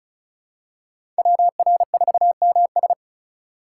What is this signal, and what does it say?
Morse code, heard as a keyed tone, about 36 wpm, 700 Hz. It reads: WR4MS
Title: USA Call Signs - 5 Characters - 3x Character Spacing - 35wpm